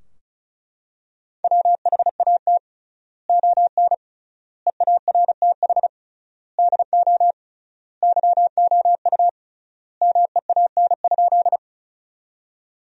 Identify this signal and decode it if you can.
Morse code, heard as a keyed tone, about 36 wpm, 700 Hz. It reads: WHAT ON EARTH DO YOU MEAN?
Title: 3x Word-Spacing - Sentences from Top 200 Words - 35wpm